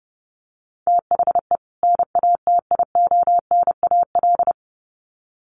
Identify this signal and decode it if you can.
Morse code, heard as a keyed tone, about 30 wpm, 700 Hz. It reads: THE NATIONAL
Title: Sets of 2 Words 30wpm